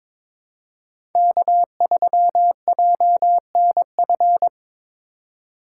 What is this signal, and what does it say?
Morse code, heard as a keyed tone, about 22 wpm, 700 Hz. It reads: K3JNF
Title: USA Call Signs - 3x Character Spacing - 22wpm